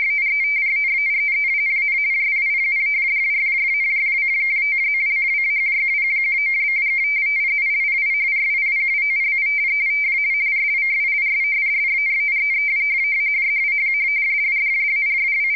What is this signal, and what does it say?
Two-tone teleprinter signal, about 170 Hz shift.
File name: ascii170 / ascii50